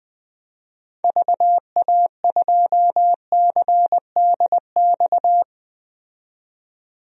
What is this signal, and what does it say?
Morse code, heard as a keyed tone, about 21 wpm, 700 Hz. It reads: VA2CDX
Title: DXCC-100 Call Signs - 6 Characters - 20wpm